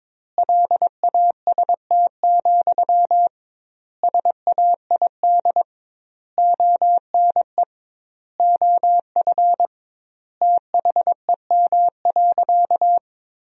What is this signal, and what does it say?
Morse code, heard as a keyed tone, about 22 wpm, 700 Hz. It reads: LAST, SAID ONE OF THEM.
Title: 2x Word-Spacing - Sentences from Top 300 Words - 22wpm